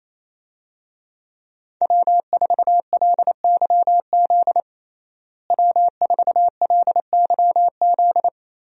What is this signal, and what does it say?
Morse code, heard as a keyed tone, about 28 wpm, 700 Hz. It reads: W4LYZ W4LYZ
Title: USA Call Signs - 5 Characters - Repeated 3x - 28wpm